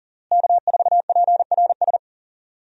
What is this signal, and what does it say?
Morse code, heard as a keyed tone, about 40 wpm, 700 Hz. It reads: K4PRS
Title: USA Call Signs - 5 Characters - Repeated 3x - 40wpm